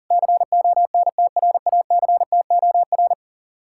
Morse code, heard as a keyed tone, about 48 wpm, 700 Hz. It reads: CONTRACTOR
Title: Speed-Racing Common QSO Phrases - 40wpm